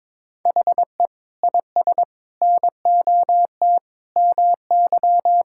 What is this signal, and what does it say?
Morse code, heard as a keyed tone, about 22 wpm, 700 Hz. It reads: HE IS NOT MY
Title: Sentences from Top 1600 Words 22wpm